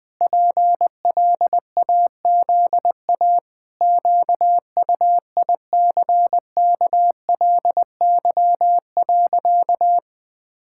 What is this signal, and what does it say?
Morse code, heard as a keyed tone, about 20 wpm, 700 Hz. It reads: PLAZA QUICKLY.
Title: Pangram Challenge v2 - 20wpm